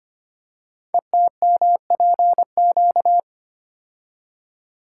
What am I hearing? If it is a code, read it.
Morse code, heard as a keyed tone, about 25 wpm, 700 Hz. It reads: ETMPQ
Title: Code Groups - 5 Letters - 25wpm